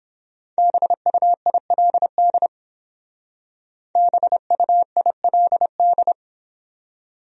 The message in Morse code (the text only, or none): BUILD BUILD